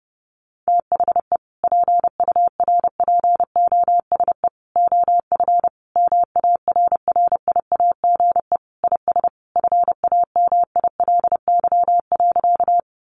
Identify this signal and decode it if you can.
Morse code, heard as a keyed tone, about 30 wpm, 700 Hz. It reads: THE PURPOSE OF MARRIAGE IS FAMILY.
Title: Sentences from Top 800 Words 30wpm